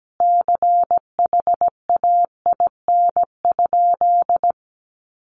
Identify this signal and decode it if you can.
Morse code, heard as a keyed tone, about 17 wpm, 700 Hz. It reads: CHAIN?
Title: Sentences from Top 1500 Words 17wpm